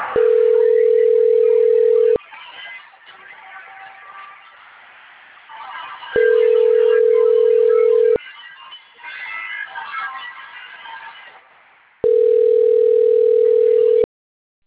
Ringback tone (North America), 440 + 480 Hz.